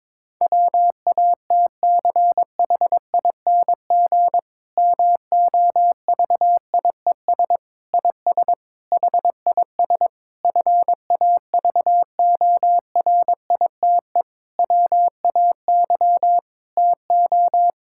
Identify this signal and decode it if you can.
Morse code, heard as a keyed tone, about 22 wpm, 700 Hz. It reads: WATCHING MOVIES IS HIS FAVORITE WAY TO
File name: Difficult Sentences with Hints - 22wpm